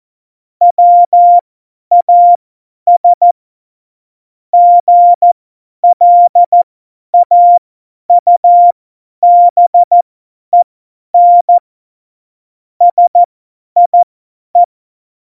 Morse code, heard as a keyed tone, about 14 wpm, spaced as for 10 wpm, 700 Hz. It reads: WAS GLAUBEN SIE
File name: Morsekurs / Witze / Witz 15. September 2020, 50 F70